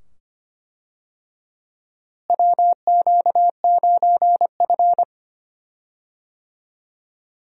Morse code, heard as a keyed tone, about 25 wpm, 700 Hz. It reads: WQ9F
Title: USA Call Signs - 4 Characters - Repeated 2x - 25wpm